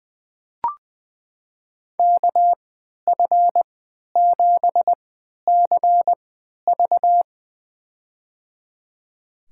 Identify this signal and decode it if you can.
Morse code, heard as a keyed tone, about 20 wpm, spaced as for 11 wpm, 700 Hz. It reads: KF7CV
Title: USA Call Signs - 5 Characters - 3x Character Spacing - 20wpm